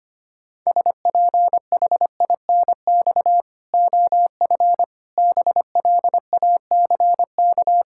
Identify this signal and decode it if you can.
Morse code, heard as a keyed tone, about 25 wpm, 700 Hz. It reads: SPHINX OF BLACK
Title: Pangram Challenge v2 - 25wpm